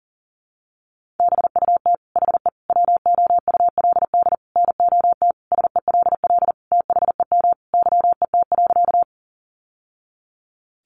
Morse code, heard as a keyed tone, about 40 wpm, 700 Hz. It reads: BUT HE WOULD NOT SELL THEM YET.
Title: Sentences from Top 1100 Words 40wpm